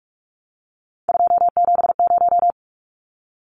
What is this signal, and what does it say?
Morse code, heard as a keyed tone, about 44 wpm, 700 Hz. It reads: 270
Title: Numbers - 3-Digit 45wpm